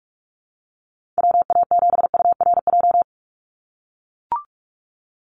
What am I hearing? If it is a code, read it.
Morse code, heard as a keyed tone, about 44 wpm, 700 Hz. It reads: WA7URJ
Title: USA Call Signs 45wpm